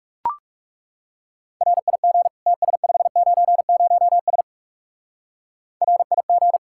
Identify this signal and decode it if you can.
Morse code, heard as a keyed tone, about 36 wpm, 700 Hz. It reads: RIG TS590S RIG
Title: Speed-Racing Common QSO Phrases - 30wpm